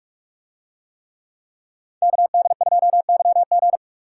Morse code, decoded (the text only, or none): KD2YG